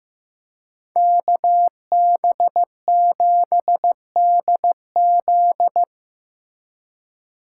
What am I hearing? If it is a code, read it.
Morse code, heard as a keyed tone, about 18 wpm, 700 Hz. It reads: KB7DZ
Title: Speed Racing - USA 2x2 Call Signs 15wpm